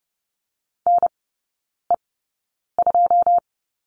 Morse code, heard as a keyed tone, about 30 wpm, spaced as for 9 wpm, 700 Hz. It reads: NE2
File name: 081 Lesson 14e - Number 2 - Three Characters 30wpm